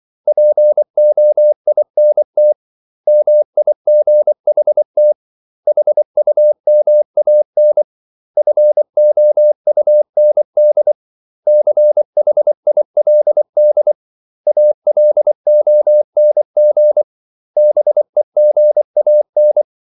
Morse code, heard as a keyed tone, about 24 wpm, 600 Hz. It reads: POINT MIGHT HUMAN FOUND CHILD ALONG BEGAN
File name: CW / Practice / 500MostCommonEnglishWords24